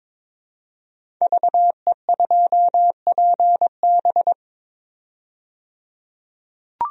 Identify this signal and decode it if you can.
Morse code, heard as a keyed tone, about 22 wpm, 700 Hz. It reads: VE2PB
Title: DXCC-100 Call Signs - 5 Characters - 22wpm